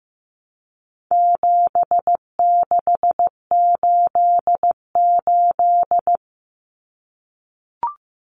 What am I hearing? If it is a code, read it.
Morse code, heard as a keyed tone, about 15 wpm, 700 Hz. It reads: 7688
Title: Common Cut Numbers - 4-Digit 15wpm